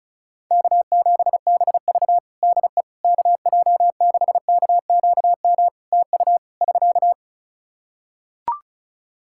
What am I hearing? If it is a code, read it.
Morse code, heard as a keyed tone, about 36 wpm, 700 Hz. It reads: K7BV DE KJ6KQM TU <SK>
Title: Practice QSO - Exchange 4 - Ending - 35wpm